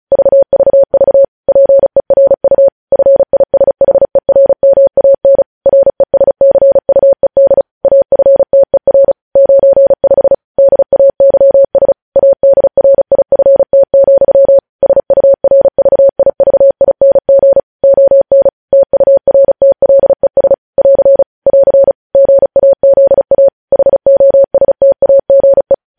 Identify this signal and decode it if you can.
Morse code, heard as a keyed tone, about 36 wpm, 550 Hz. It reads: VVV PERU FISHERMAN RESCUED AFTER 95 DAYS ADRIFT, SURVIVING ON TURTLES <AR> <AR> GAZA HOSTAGE
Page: News Headlines in Morse Code at 35 WPM